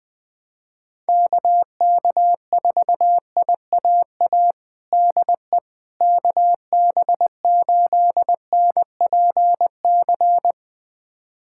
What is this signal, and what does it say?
Morse code, heard as a keyed tone, about 20 wpm, 700 Hz. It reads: KK4IAA DE KB8NPC
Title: USA Callsign DE Callsign - 6 Characters - 20wpm